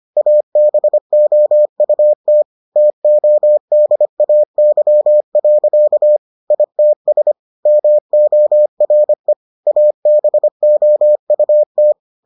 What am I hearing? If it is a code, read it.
Morse code, heard as a keyed tone, about 25 wpm, 600 Hz. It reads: ABOUT TODAY. ITS MORE ABOUT